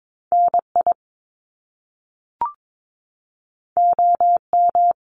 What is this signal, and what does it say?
Morse code, heard as a keyed tone, about 22 wpm, 700 Hz. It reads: NI OM
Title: Top Two Letter Combinations in Words 22wpm